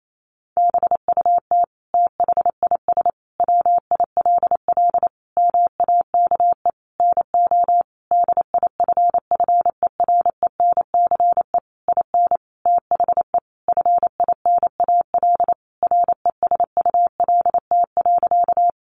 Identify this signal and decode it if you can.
Morse code, heard as a keyed tone, about 28 wpm, 700 Hz. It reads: BUT THIS WILL MAKE NO DIFFERENCE IN THE FINAL RESULT.
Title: Sentences from Top 800 Words 28wpm